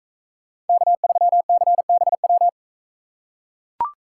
Morse code, heard as a keyed tone, about 44 wpm, 700 Hz. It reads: K3CBW